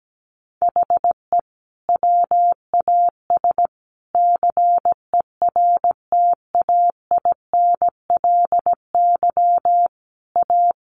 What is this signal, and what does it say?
Morse code, heard as a keyed tone, about 17 wpm, 700 Hz. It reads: HE WAS CERTAINLY A
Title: Sentences from Top 1200 Words 17wpm